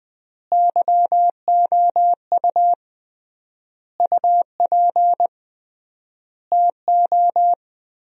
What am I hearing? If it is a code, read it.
Morse code, heard as a keyed tone, about 21 wpm, 700 Hz. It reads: YOU UP TO
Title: 3x Word-Spacing - Sentences from Top 100 Words - 20wpm